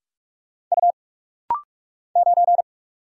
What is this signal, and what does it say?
Morse code, heard as a keyed tone, about 46 wpm, 700 Hz. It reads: U 9
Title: Single Letter-Number Alt - Rapid-Fire 45wpm